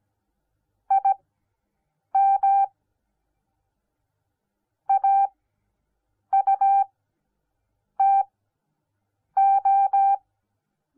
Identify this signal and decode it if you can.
Morse code, heard as a keyed tone, about 17 wpm, spaced as for 6 wpm, 790 Hz. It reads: IM AUTO